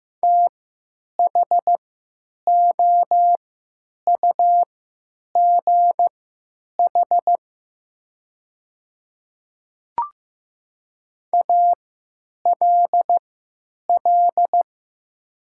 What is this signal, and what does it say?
Morse code, heard as a keyed tone, about 15 wpm, spaced as for 9 wpm, 700 Hz. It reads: THOUGH ALL
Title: 3x Character Spacing - Top 500 Words - 15wpm